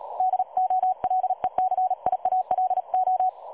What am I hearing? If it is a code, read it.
Morse code, heard as a keyed tone, about 36 wpm, 740 Hz. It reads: DOLECIALO
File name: dlazaawansowanych / nagraniazpasma / websdr recording 3565.0kHz